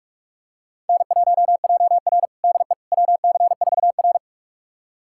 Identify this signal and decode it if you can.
Morse code, heard as a keyed tone, about 41 wpm, 700 Hz. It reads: N1JR DE WC4R
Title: USA Callsign DE Callsign - 4 Characters - 45wpm